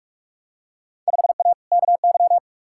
Morse code, heard as a keyed tone, about 42 wpm, 700 Hz. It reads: 5A KY
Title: Field Day Exchanges - 45wpm